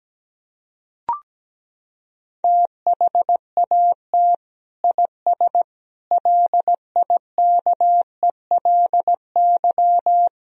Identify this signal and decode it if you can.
Morse code, heard as a keyed tone, about 17 wpm, 700 Hz. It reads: THAT IS LIKELY
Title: Sentences from Top 1000 Words - Review Part 4 - 17wpm